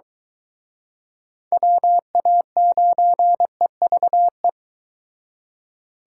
Morse code, heard as a keyed tone, about 23 wpm, 700 Hz. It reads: WA9EVE